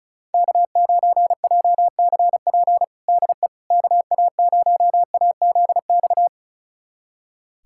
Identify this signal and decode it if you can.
Morse code, heard as a keyed tone, about 34 wpm, 700 Hz. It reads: K9JCP DE KA0AZX
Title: USA Callsign DE Callsign 35wpm